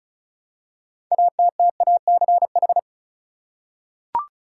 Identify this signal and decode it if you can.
Morse code, heard as a keyed tone, about 36 wpm, 700 Hz. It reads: ATTACH
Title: Six Letter Words - 35wpm